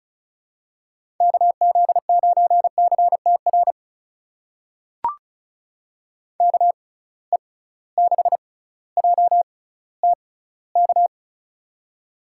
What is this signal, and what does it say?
Morse code, heard as a keyed tone, about 36 wpm, spaced as for 18 wpm, 700 Hz. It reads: KZ9CTR KE6JTK
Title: USA Call Signs - 6x Character Spacing - 35wpm